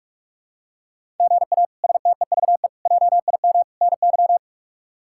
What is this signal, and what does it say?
Morse code, heard as a keyed tone, about 43 wpm, 700 Hz. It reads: GA STEVE JIM NY
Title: K1USN SST QSO Practice - 45wpm